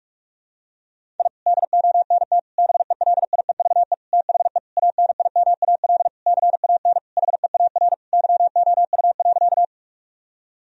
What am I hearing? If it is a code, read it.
Morse code, heard as a keyed tone, about 41 wpm, 700 Hz. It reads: I DONT BELIEVE THE ANIMAL CAN HEAR YOU.
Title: Sentences from Top 1000 Words - Review Part 1 - 45wpm